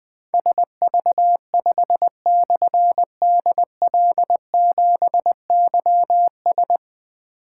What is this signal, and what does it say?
Morse code, heard as a keyed tone, about 20 wpm, 700 Hz. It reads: SV5/DL7YS
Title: Difficult Call Signs 20wpm